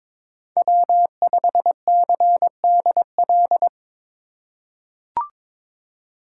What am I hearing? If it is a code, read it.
Morse code, heard as a keyed tone, about 22 wpm, 700 Hz. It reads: W5CDL
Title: USA Call Signs - 5 Characters - 4x Character Spacing - 22wpm